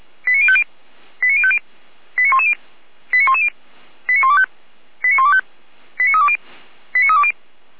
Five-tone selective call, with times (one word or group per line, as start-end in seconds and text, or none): 0.3-0.6 89050
1.2-1.6 89050
2.2-2.5 89110
3.1-3.5 89110
4.1-4.5 89125
5.0-5.4 89125
6.0-6.4 89320
7.0-7.3 89320